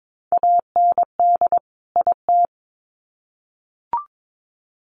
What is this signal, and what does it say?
Morse code, heard as a keyed tone, about 22 wpm, 700 Hz. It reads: AND IT
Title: Sets of 2 Words 22wpm